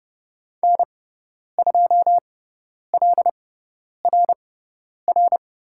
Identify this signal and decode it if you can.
Morse code, heard as a keyed tone, about 30 wpm, spaced as for 10 wpm, 700 Hz. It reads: N2LRR
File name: Lesson 17g - Letter u - Call Signs 30wpm